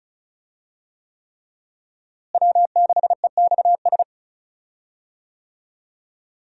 Morse code, heard as a keyed tone, about 36 wpm, 700 Hz. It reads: W6EXS
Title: USA Call Signs - 5 Characters - Repeated 3x - 35wpm